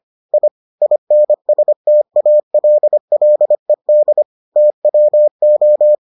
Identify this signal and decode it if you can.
Morse code, heard as a keyed tone, about 25 wpm, 600 Hz. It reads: I INSTALLED TWO